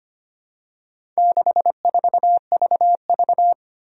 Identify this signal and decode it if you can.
Morse code, heard as a keyed tone, about 25 wpm, 700 Hz. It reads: 64VV
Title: Four Challenging Letter-Number Combinations 25wpm